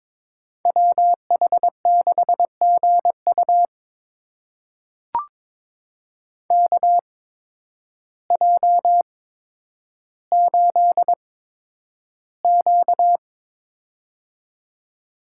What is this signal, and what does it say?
Morse code, heard as a keyed tone, about 22 wpm, spaced as for 10 wpm, 700 Hz. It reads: WH6GU KJ8Q
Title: USA Call Signs - 8x Character Spacing - 22wpm